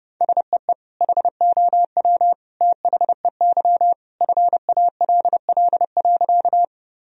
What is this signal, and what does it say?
Morse code, heard as a keyed tone, about 30 wpm, 700 Hz. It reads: SEE HOW THEY FALL.
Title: Sentences from Top 1000 Words - Review Part 2 - 30wpm